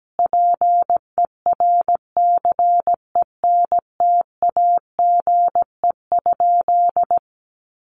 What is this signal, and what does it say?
Morse code, heard as a keyed tone, about 17 wpm, 700 Hz. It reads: PERCENTAGE?
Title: Sentences from Top 1300 Words 17wpm